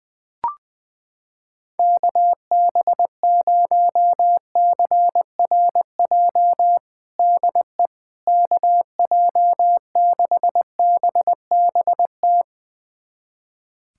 Morse code, heard as a keyed tone, about 20 wpm, 700 Hz. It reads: KB0CRJ DE KJ6BBT
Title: USA Callsign DE Callsign - 6 Characters - 20wpm